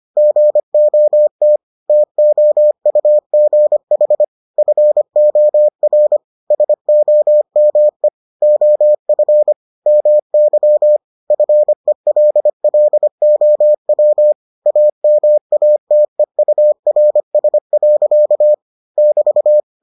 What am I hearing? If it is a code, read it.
Morse code, heard as a keyed tone, about 25 wpm, 600 Hz. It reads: GOT TOUGH FOR SOME OF MY FELLOW AMATEURS. <BT>